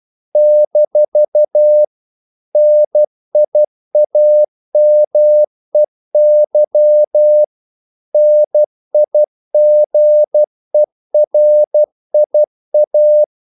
Morse code, heard as a keyed tone, about 12 wpm, 600 Hz. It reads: - NIAMEY NIGERIA
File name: CW / Practice / NationalCapitals12